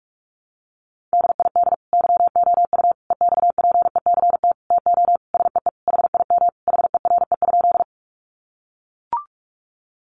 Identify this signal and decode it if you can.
Morse code, heard as a keyed tone, about 44 wpm, 700 Hz. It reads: DID YOU EXPECT TO SEE HIM HERE?
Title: Sentences from Top 1100 Words 45wpm